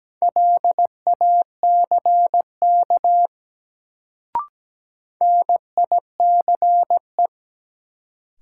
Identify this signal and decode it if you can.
Morse code, heard as a keyed tone, about 17 wpm, 700 Hz. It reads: LACK NICE
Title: Sentences from Top 1100 Words - New Word Drill Alt - 17wpm